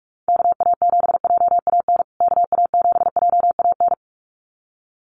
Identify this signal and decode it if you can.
Morse code, heard as a keyed tone, about 44 wpm, 700 Hz. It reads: KA7JAN KA7JAN
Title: USA Call Signs Repeated 2x 45wpm